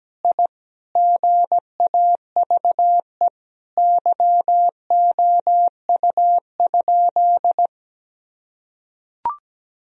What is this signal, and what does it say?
Morse code, heard as a keyed tone, about 17 wpm, 700 Hz. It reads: I GAVE YOU?
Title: Sentences from Top 1000 Words - Review Part 4 - 17wpm